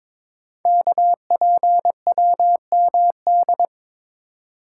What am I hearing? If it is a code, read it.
Morse code, heard as a keyed tone, about 22 wpm, 700 Hz. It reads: KPWMD
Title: Code Groups - 5 Letters - 22wpm